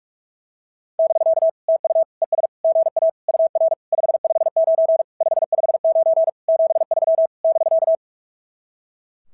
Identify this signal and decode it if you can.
Morse code, heard as a keyed tone, about 44 wpm, 660 Hz. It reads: <BK> TU ES GA UR 559 559 73 <BK>
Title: SOTA QSO Practice 45wpm